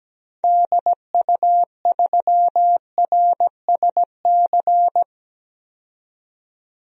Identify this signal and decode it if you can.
Morse code, heard as a keyed tone, about 17 wpm, 700 Hz. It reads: DU3RSC
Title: DXCC-100 Call Signs - 6 Characters - 17wpm